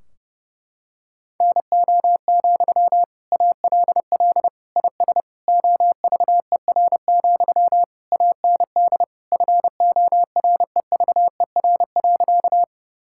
Morse code, heard as a keyed tone, about 30 wpm, 700 Hz. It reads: NO, ALL IS OVER, AND FOREVER.
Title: Sentences from Top 1600 Words 30wpm